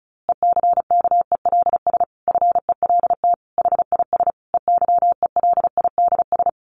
Morse code, heard as a keyed tone, about 34 wpm, 700 Hz. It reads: ECKELS FELT HIS EYELIDS
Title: A Sound Of Thunder 35wpm